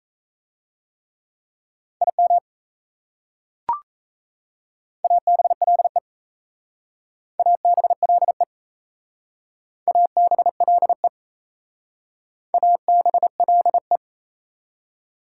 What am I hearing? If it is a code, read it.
Morse code, heard as a keyed tone, about 36 wpm, 700 Hz. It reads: IM ABLE ABLE ABLE ABLE